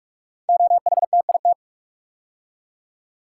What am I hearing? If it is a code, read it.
Morse code, heard as a keyed tone, about 44 wpm, 700 Hz. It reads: OHTIT
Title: Code Groups - 5 Letters - 45wpm